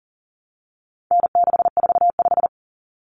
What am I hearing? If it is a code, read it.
Morse code, heard as a keyed tone, about 40 wpm, 700 Hz. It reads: N645
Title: Common Cut Numbers - 4-Digit 40wpm